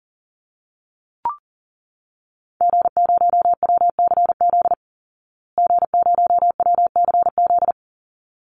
Morse code, heard as a keyed tone, about 40 wpm, 700 Hz. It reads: G0WCZ G0WCZ
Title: Call Signs Repeated 3x - 40wpm